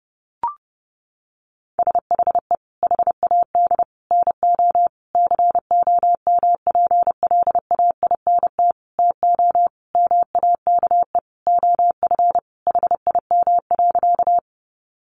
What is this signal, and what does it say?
Morse code, heard as a keyed tone, about 30 wpm, 700 Hz. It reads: SHE HAD NO COMPLAINT TO MAKE OF HIM.
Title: Sentences from Top 1400 Words 30wpm